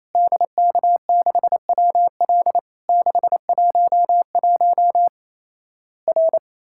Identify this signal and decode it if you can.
Morse code, heard as a keyed tone, about 27 wpm, 700 Hz. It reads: DK6WL 611 R
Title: Worked All Europe DX Contest - QTC Exchange - 28wpm